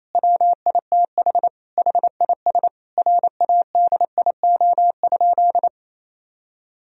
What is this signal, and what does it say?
Morse code, heard as a keyed tone, about 28 wpm, 700 Hz. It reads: WITH HIS RADIO?
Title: Sentences from Top 1000 Words - Review Part 3 - 28wpm